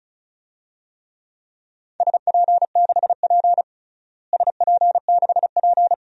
Morse code, heard as a keyed tone, about 36 wpm, 700 Hz. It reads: SP6P SP6P
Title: DXCC-100 Call Signs - 4 Characters - Repeated 2x - 35wpm